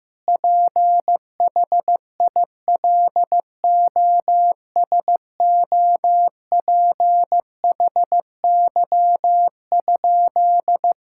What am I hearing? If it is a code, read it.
Morse code, heard as a keyed tone, about 15 wpm, 700 Hz. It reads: PHILOSOPHY?
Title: Sentences from Top 1000 Words - Review Part 6 - 15wpm